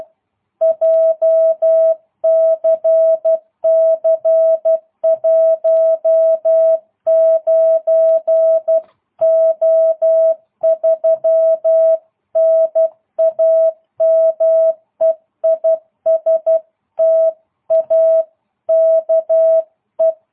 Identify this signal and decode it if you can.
Morse code, heard as a keyed tone, about 12 wpm, 660 Hz. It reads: JCC 19O3 NAME IS TAKE